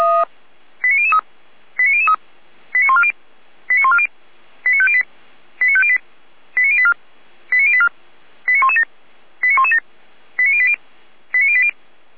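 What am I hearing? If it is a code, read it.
Five-tone selective call (ZVEI), heard as quick runs of five tones, tones of about 70 ms. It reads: 89003 89003 89140 89140 89508 89508 89084 89084 89117 89117 89080 89080
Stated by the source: ZVEI five-tone selective call received on an SDR.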